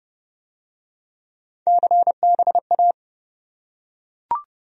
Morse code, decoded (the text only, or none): CBA